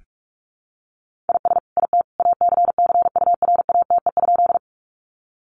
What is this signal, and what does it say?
Morse code, heard as a keyed tone, about 44 wpm, 700 Hz. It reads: IS IT ACCURATE?
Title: Sentences from Top 1200 Words 45wpm